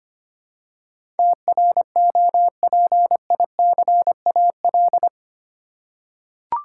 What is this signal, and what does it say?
Morse code, heard as a keyed tone, about 25 wpm, 700 Hz. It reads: TROPICAL
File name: 8 Letter Words - First 3 Letters Given - 25wpm